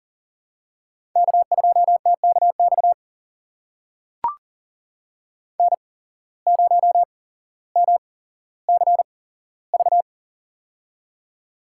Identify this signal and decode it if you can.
Morse code, heard as a keyed tone, about 40 wpm, spaced as for 17 wpm, 700 Hz. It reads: K2TKX N0MCV
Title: USA Call Signs - 5 Characters - 8x Character Spacing - 40wpm